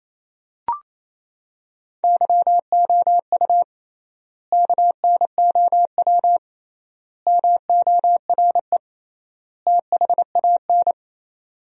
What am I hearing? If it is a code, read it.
Morse code, heard as a keyed tone, about 28 wpm, 700 Hz. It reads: YOU KNOW MORE THAN
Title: 3x Word-Spacing - Sentences from Top 100 Words - 28wpm